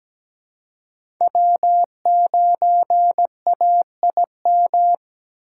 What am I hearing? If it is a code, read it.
Morse code, heard as a keyed tone, about 20 wpm, 700 Hz. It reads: W9AIM
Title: Speed Racing - USA 1x3 Call Signs 17wpm